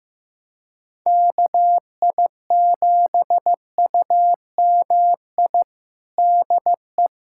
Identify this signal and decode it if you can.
Morse code, heard as a keyed tone, about 15 wpm, 700 Hz. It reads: KI7UMI DE
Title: USA Callsign DE Callsign - 6 Characters - 15wpm